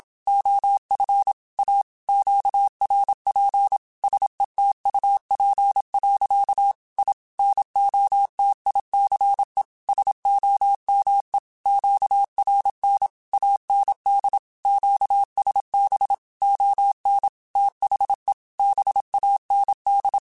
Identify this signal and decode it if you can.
Morse code, heard as a keyed tone, about 27 wpm, 800 Hz. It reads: OF A QRP SETUP. I NOTICE SOME QRN AND QSB ON THE BAND
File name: QSOs / QSOs-mp3s / WD0BGP 1@25